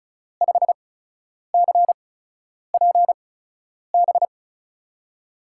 Morse code, heard as a keyed tone, about 36 wpm, spaced as for 10 wpm, 700 Hz. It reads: 5CPB
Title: USA Call Signs - 8x Character Spacing - 35wpm